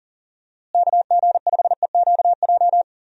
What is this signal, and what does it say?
Morse code, heard as a keyed tone, about 40 wpm, 700 Hz. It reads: KG5EQJ
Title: USA Call Signs - 6 Characters - Repeated 3x - 40wpm